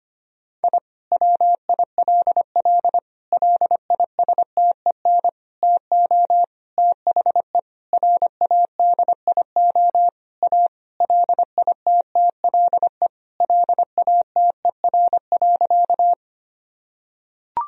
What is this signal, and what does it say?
Morse code, heard as a keyed tone, about 25 wpm, 700 Hz. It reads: I WILL LISTEN TO THE RADIO A LITTLE LATER.
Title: Sentences from Top 1000 Words - Review Part 3 - 25wpm